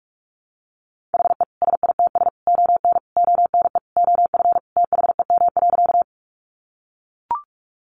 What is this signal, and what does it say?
Morse code, heard as a keyed tone, about 44 wpm, 700 Hz. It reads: HE SITS ON ONE OF THEM.
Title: Additional Easy Phrases 45wpm